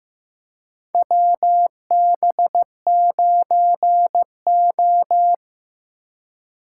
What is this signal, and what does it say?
Morse code, heard as a keyed tone, about 15 wpm, 700 Hz. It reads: WB9O
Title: USA Call Signs - 4 Characters - 2x Character Spacing - 15wpm